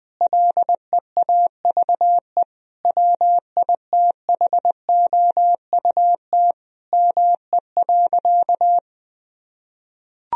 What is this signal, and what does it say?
Morse code, heard as a keyed tone, about 21 wpm, 700 Hz. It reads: LEAVE WITHOUT ME.
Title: Sentences from Top 1000 Words - Review Part 3 - 20wpm